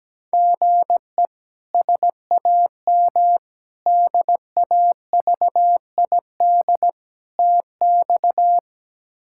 Morse code, heard as a keyed tone, about 17 wpm, 700 Hz. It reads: GE SAM DAVID TX
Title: K1USN SST QSO Practice - 17wpm